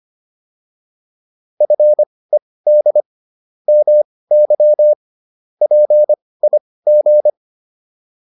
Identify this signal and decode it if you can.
Morse code, heard as a keyed tone, about 25 wpm, spaced as for 18 wpm, 600 Hz. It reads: FED MY PIG